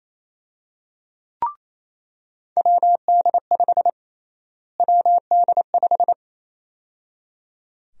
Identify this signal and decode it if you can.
Morse code, heard as a keyed tone, about 28 wpm, 700 Hz. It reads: WD5 WD5
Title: USA Call Sign Prefixes - Repeated 2x - 28wpm